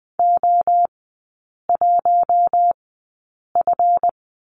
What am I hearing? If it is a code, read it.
Morse code, heard as a keyed tone, about 19 wpm, spaced as for 8 wpm, 700 Hz. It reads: O1F
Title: 138 Lesson 23e - Letter f - Three Characters 20wpm